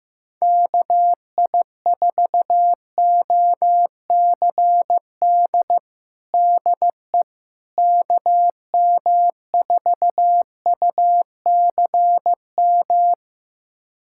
Morse code, heard as a keyed tone, about 16 wpm, 700 Hz. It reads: KI4OCD DE KM4UCM
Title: USA Callsign DE Callsign - 6 Characters - 15wpm